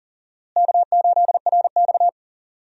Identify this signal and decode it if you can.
Morse code, heard as a keyed tone, about 39 wpm, 700 Hz. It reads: K8RX
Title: USA Call Signs - 4 Characters - Repeated 2x - 40wpm